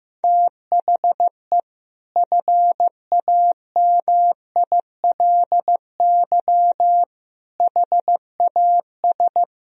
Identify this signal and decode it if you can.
Morse code, heard as a keyed tone, about 15 wpm, 700 Hz. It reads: THE FAMILY HAS
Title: Sentences from Top 1000 Words - Review Part 2 - 15wpm